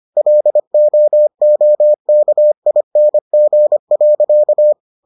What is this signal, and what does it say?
Morse code, heard as a keyed tone, about 25 wpm, 600 Hz. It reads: LOOKING.